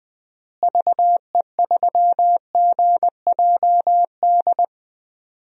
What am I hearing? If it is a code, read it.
Morse code, heard as a keyed tone, about 20 wpm, 700 Hz. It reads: VE3GJD
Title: DXCC-100 Call Signs - 6 Characters - Repeated 2x - 20wpm